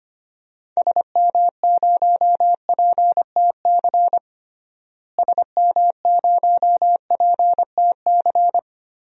Morse code, heard as a keyed tone, about 25 wpm, 700 Hz. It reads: SM0PTC SM0PTC
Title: DXCC-100 Call Signs - 6 Characters - Repeated 3x - 25wpm